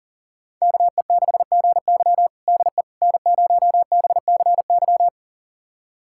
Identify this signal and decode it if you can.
Morse code, heard as a keyed tone, about 40 wpm, 700 Hz. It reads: KE6GY DE N0BCY
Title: USA Callsign DE Callsign 40wpm